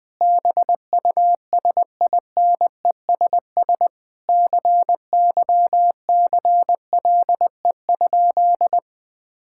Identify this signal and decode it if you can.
Morse code, heard as a keyed tone, about 19 wpm, 700 Hz. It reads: BUSINESS CYCLE?
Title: Sentences from Top 1000 Words - Review Part 6 - 20wpm